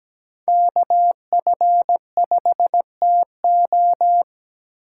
Morse code, heard as a keyed tone, about 17 wpm, 700 Hz. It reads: KF5TO
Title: USA Call Signs - 8x Character Spacing - 17wpm